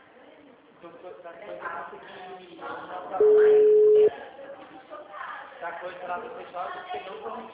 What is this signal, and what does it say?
Ringback tone (Europe, 425 Hz), with speech.